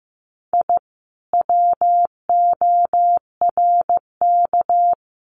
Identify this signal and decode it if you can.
Morse code, heard as a keyed tone, about 15 wpm, 700 Hz. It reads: I WORK
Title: Common QSO Phrases Encore 15wpm